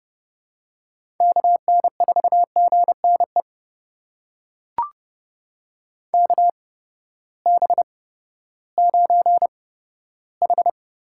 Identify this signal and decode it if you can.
Morse code, heard as a keyed tone, about 30 wpm, spaced as for 15 wpm, 700 Hz. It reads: KN4GNE KB9H
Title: USA Call Signs - 6 Characters - 8x Character Spacing - 30wpm